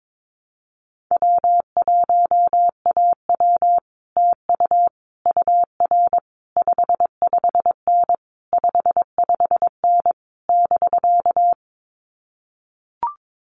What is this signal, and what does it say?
Morse code, heard as a keyed tone, about 22 wpm, 700 Hz. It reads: W1AW TU UR 55N 55N <BK>
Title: Common QSO Phrases Encore 22wpm